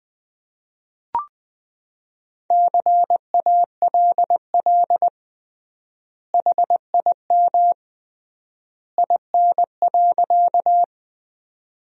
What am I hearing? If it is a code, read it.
Morse code, heard as a keyed tone, about 20 wpm, 700 Hz. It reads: CALL HIM IN.
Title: 3x Word-Spacing - Sentences from Top 100 Words - 20wpm